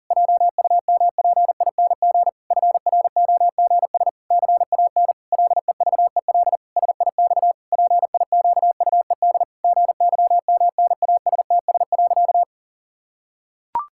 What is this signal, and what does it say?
Morse code, heard as a keyed tone, about 40 wpm, 700 Hz. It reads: JUMPING FROGS CAN LEVEL SIX PIQUED GYMNASTS.
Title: Pangram Challenge v2 - 40wpm